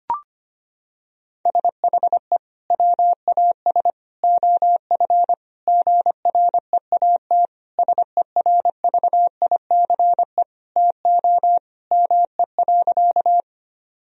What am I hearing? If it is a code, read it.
Morse code, heard as a keyed tone, about 25 wpm, 700 Hz. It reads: SHE WAS OF GREAT SERVICE TO ME.
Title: Sentences from Top 1000 Words - Review Part 4 - 25wpm